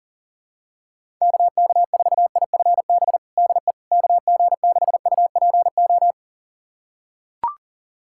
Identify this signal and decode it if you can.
Morse code, heard as a keyed tone, about 40 wpm, 700 Hz. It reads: KK4IFB DE KG6UPO
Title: USA Callsign DE Callsign - 6 Characters - 40wpm